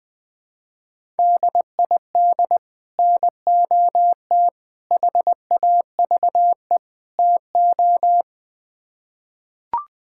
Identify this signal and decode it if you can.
Morse code, heard as a keyed tone, about 20 wpm, 700 Hz. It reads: DID NOT HAVE TO
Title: Build A Phrase - 4-Word Phrases - 20wpm